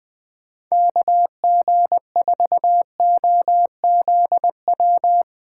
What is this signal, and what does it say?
Morse code, heard as a keyed tone, about 20 wpm, 700 Hz. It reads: KG4OZW
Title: USA Call Signs - 6 Characters - 6x Character Spacing - 20wpm